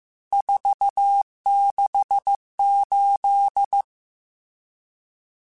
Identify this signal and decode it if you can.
Morse code, heard as a keyed tone, about 15 wpm, 800 Hz. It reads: 468
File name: RSTs / RST-mp3s / Various RST @13